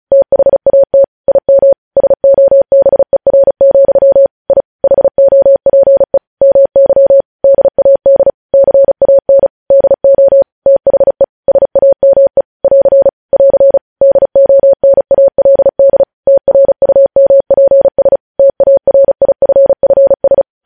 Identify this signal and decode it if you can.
Morse code, heard as a keyed tone, about 36 wpm, 550 Hz. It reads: THAT IM SOBER, I HOPE MY DAD CAN DO THE SAME <AR> <AR> DONALD TRUMPS TARIFFS
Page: news headlines in morse code at 35 wpm